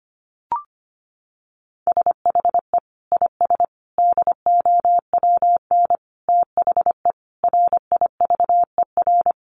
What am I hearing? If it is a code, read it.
Morse code, heard as a keyed tone, about 25 wpm, 700 Hz. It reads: SHE IS DOWN THE RIVER
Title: Sentences from Top 1400 Words 25wpm